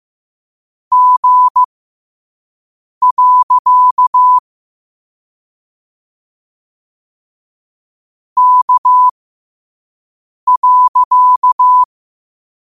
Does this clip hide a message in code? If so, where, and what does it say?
Yes: Morse code from 0.9 to 11.8 s, heard as a keyed tone, about 15 wpm, spaced as for 5 wpm, 1000 Hz. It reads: G. K.